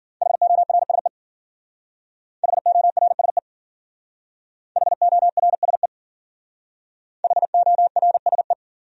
Morse code, heard as a keyed tone, about 54 wpm, 700 Hz. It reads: HORSE HORSE HORSE HORSE